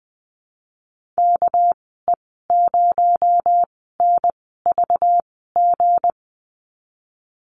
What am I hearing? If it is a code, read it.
Morse code, heard as a keyed tone, about 20 wpm, spaced as for 15 wpm, 700 Hz. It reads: KE0NVG
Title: 245 Lesson 39g - Letter z - Call Signs 20wpm